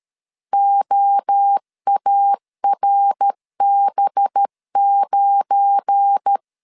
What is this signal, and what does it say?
Morse code, heard as a keyed tone, about 13 wpm, 800 Hz. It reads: OARB9
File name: CW / DET / 1 9 1500C S A 12wpm 3800